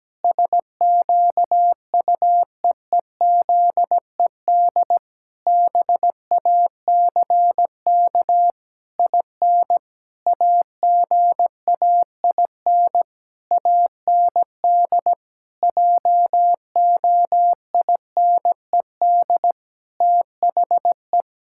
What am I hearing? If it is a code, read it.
Morse code, heard as a keyed tone, about 17 wpm, 700 Hz. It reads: SQUEEZED BACK IN AGAIN AND JOINED THE
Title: Pangram Challenge v2 - 17wpm